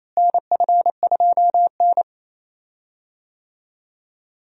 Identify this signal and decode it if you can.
Morse code, heard as a keyed tone, about 28 wpm, 700 Hz. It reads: NF2N
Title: USA Call Signs - 4 Characters - Repeated 3x - 28wpm